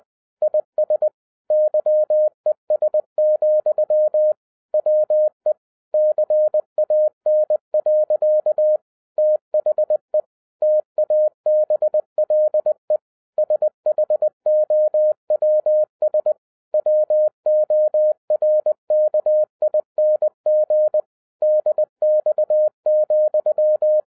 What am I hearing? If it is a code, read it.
Morse code, heard as a keyed tone, about 20 wpm, 610 Hz. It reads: IS YES, WE CAN. THE TABLE SHOWS WORKING DX,
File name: 2012-01-mp3 / n15-INVESTIGATION INTO THE POSSIBILITY OF WINNING PREFIX VERSION 3 AWARD-20wpm